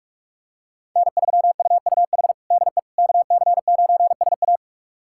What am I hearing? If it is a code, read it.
Morse code, heard as a keyed tone, about 42 wpm, 700 Hz. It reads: N3UUH DE KC9SA
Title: USA Callsign DE Callsign - 5 Characters - 45wpm